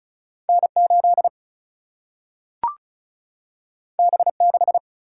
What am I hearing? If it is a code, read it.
Morse code, heard as a keyed tone, about 36 wpm, 700 Hz. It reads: N 8 B 6
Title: Single-Letter Single-Number 35wpm